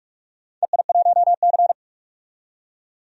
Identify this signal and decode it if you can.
Morse code, heard as a keyed tone, about 44 wpm, 700 Hz. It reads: EI1C
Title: DXCC-100 Call Signs - 4 Characters - 45wpm